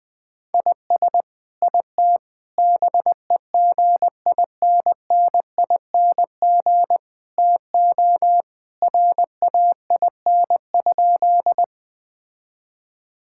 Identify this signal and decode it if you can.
Morse code, heard as a keyed tone, about 19 wpm, 700 Hz. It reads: IS IT BEGINNING TO RAIN?
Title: Sentences from Top 1000 Words - Review Part 5 - 20wpm